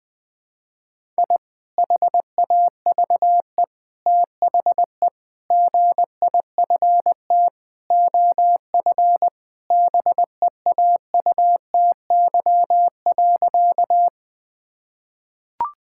Morse code, heard as a keyed tone, about 20 wpm, 700 Hz. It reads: I HAVE THE GIFT OF BEAUTY.
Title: Sentences from Top 1000 Words - Review Part 6 - 20wpm